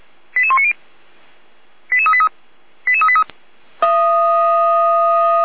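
Five-tone selective call (ZVEI), heard as quick runs of five tones, tones of about 70 ms. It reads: 88190 88383 88383
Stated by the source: Received on SDR; ZVEI five-tone call, feuerwehr-orlamuende.org (fire brigade Orlamünde).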